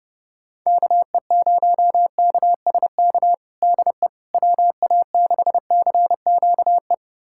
Morse code, heard as a keyed tone, about 30 wpm, 700 Hz. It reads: KE0KSK DE WA6CQE
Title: USA Callsign DE Callsign 30wpm